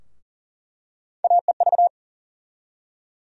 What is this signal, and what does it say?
Morse code, heard as a keyed tone, about 40 wpm, 700 Hz. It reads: AEV
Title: USA Call Sign Postfixes - 3 Letters 40wpm